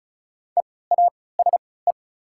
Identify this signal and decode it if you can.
Morse code, heard as a keyed tone, about 36 wpm, spaced as for 20 wpm, 690 Hz. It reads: EASE
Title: 3x Character Spacing - Top 300 Words - 35wpm